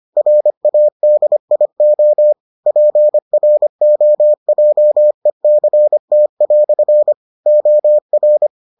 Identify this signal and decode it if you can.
Morse code, heard as a keyed tone, about 25 wpm, 600 Hz. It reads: RADIO PROJECT" OR